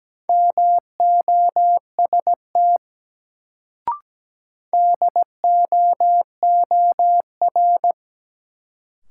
Morse code, heard as a keyed tone, about 17 wpm, 700 Hz. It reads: MOST DOOR
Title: Sets of 2 Words - New Word Drill - 17wpm